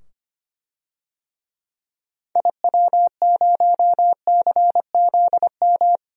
Morse code, heard as a keyed tone, about 25 wpm, 700 Hz. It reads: IW0CZM
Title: DXCC-100 Call Signs - 6 Characters - 25wpm